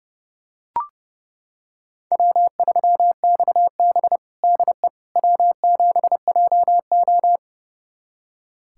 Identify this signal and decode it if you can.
Morse code, heard as a keyed tone, about 31 wpm, 700 Hz. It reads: W3XB DE W7JO
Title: USA Callsign DE Callsign - 4 Characters - 30wpm